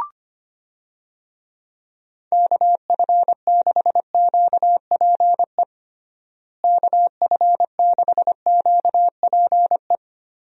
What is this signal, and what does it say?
Morse code, heard as a keyed tone, about 25 wpm, 700 Hz. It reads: KF6QPE KF6QPE
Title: USA Call Signs - 6 Characters - Repeated 3x - 25wpm